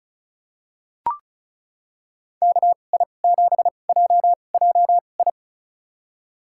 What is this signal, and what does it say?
Morse code, heard as a keyed tone, about 34 wpm, spaced as for 25 wpm, 700 Hz. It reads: KI7JJI
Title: USA Call Signs - 2x Character Spacing - 35wpm